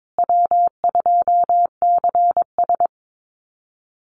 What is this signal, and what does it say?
Morse code, heard as a keyed tone, about 22 wpm, 700 Hz. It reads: W2CS
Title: Easy Call Signs Repeated 3x - 22wpm